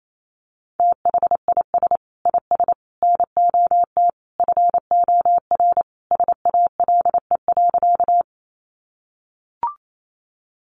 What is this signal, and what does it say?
Morse code, heard as a keyed tone, about 28 wpm, 700 Hz. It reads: THIS IS NOT FOR SALE.
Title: Sentences from Top 1400 Words 28wpm